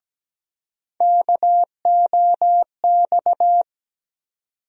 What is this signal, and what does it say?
Morse code, heard as a keyed tone, about 17 wpm, 700 Hz. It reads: KOX
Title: USA Call Sign Postfixes - 3 Letters 17wpm